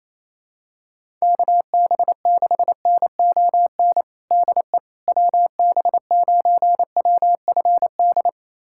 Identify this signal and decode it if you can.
Morse code, heard as a keyed tone, about 28 wpm, 700 Hz. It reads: KB6NON DE WB9WFD
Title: USA Callsign DE Callsign 28wpm